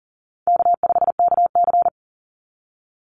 Morse code, heard as a keyed tone, about 40 wpm, 700 Hz. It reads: K5KC
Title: Call Signs - 40wpm